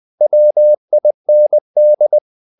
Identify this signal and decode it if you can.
Morse code, heard as a keyed tone, about 20 wpm, 600 Hz. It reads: WIND
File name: the old man and the sea part 2 20wpm